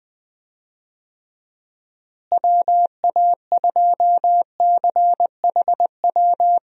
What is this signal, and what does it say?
Morse code, heard as a keyed tone, about 19 wpm, 700 Hz. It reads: WA2CHW
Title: USA Call Signs - 6 Characters - Repeated 3x - 20wpm